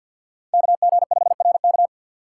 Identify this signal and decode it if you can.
Morse code, heard as a keyed tone, about 52 wpm, 700 Hz. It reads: KG5RK
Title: USA Call Signs - 5 Characters - 2x Character Spacing - 50wpm